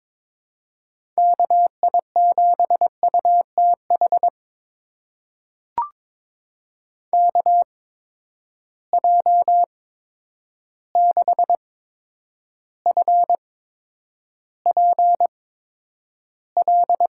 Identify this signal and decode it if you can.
Morse code, heard as a keyed tone, about 22 wpm, spaced as for 9 wpm, 700 Hz. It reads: KI7UTH KJ6FPL
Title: USA Call Signs - 6 Characters - 8x Character Spacing - 22wpm